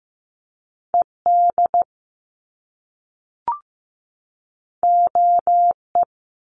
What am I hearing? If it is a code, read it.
Morse code, heard as a keyed tone, about 15 wpm, 700 Hz. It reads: ED OE
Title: Top Two Letter Combinations in Words 15wpm